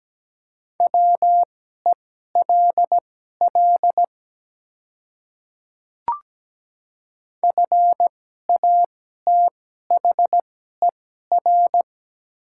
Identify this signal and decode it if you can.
Morse code, heard as a keyed tone, about 17 wpm, spaced as for 12 wpm, 700 Hz. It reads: WELL FATHER
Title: 2x Character Spacing - Top 300 Words - 17wpm